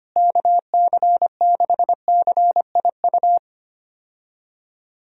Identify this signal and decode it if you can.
Morse code, heard as a keyed tone, about 25 wpm, 700 Hz. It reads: KC6CIU
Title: USA Call Signs - 6 Characters - Repeated 3x - 25wpm